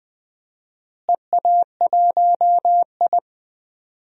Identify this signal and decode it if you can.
Morse code, heard as a keyed tone, about 20 wpm, 700 Hz. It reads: EA1I
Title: DXCC-100 Call Signs - 4 Characters - Repeated 3x - 20wpm